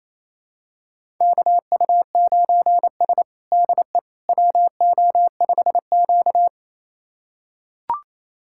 Morse code, heard as a keyed tone, about 29 wpm, 700 Hz. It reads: KU9S DE WO5Q
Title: USA Callsign DE Callsign - 4 Characters - 28wpm